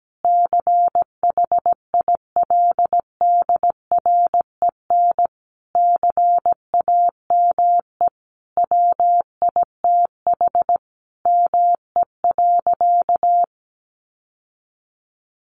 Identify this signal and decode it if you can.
Morse code, heard as a keyed tone, about 17 wpm, 700 Hz. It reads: CHILDREN CAME WITH ME.
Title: Sentences from Top 1300 Words 17wpm